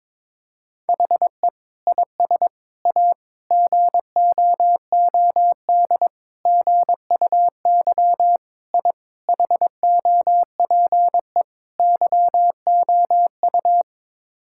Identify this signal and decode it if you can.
Morse code, heard as a keyed tone, about 22 wpm, 700 Hz. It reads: HE IS A GOOD GUY I HOPE YOU
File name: practice / 22wpm / Sets of 8 Words - 22wpm